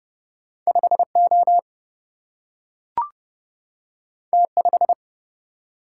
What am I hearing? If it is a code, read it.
Morse code, heard as a keyed tone, about 30 wpm, 700 Hz. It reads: 5 O T 5
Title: Single-Letter Single-Number 30wpm